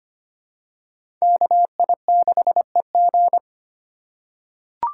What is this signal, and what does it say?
Morse code, heard as a keyed tone, about 25 wpm, 700 Hz. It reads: KI6EG
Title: USA Call Signs - 2x Character Spacing - 25wpm